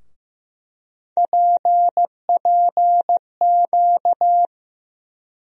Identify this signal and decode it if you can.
Morse code, heard as a keyed tone, about 15 wpm, 700 Hz. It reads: PPQ
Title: USA Call Sign Postfixes - 3 Letters 15wpm